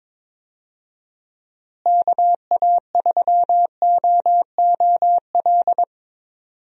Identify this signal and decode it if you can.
Morse code, heard as a keyed tone, about 22 wpm, 700 Hz. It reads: KA3OOL
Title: USA Call Signs - 6 Characters - Repeated 3x - 22wpm